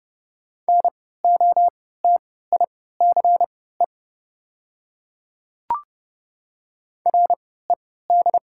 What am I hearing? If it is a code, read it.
Morse code, heard as a keyed tone, about 30 wpm, spaced as for 17 wpm, 700 Hz. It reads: NOTICE RED
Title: 3x Character Spacing - Top 500 Words - 30wpm